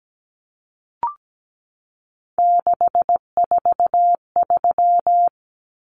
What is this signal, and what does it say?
Morse code, heard as a keyed tone, about 17 wpm, 700 Hz. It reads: 643
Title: Numbers - 3-Digit 17wpm